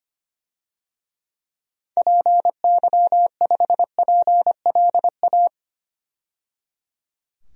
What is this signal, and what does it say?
Morse code, heard as a keyed tone, about 25 wpm, 700 Hz. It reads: PY5PLA
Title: DXCC-100 Call Signs - 6 Characters - 25wpm